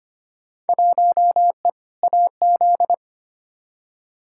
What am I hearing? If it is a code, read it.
Morse code, heard as a keyed tone, about 25 wpm, 700 Hz. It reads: 1E AZ
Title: Field Day Exchanges - 25wpm